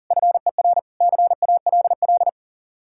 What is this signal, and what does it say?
Morse code, heard as a keyed tone, about 49 wpm, 700 Hz. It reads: FER CALL
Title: Speed-Racing Common QSO Phrases - 40wpm